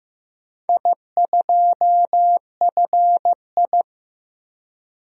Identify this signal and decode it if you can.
Morse code, heard as a keyed tone, about 15 wpm, 700 Hz. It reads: I2FI
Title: DXCC-100 Call Signs - 4 Characters - Repeated 2x - 15wpm